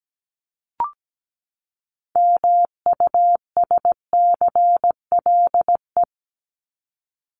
Morse code, heard as a keyed tone, about 17 wpm, 700 Hz. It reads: MUSCLE
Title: Top 1251-1500 Words 17wpm